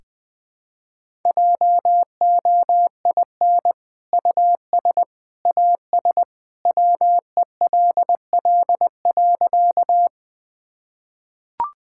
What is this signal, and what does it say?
Morse code, heard as a keyed tone, about 20 wpm, 700 Hz. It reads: JOIN US AS WELL.
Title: Sentences with 2-Word Phrases - 20wpm